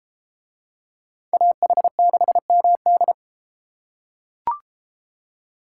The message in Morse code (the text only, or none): AH6MD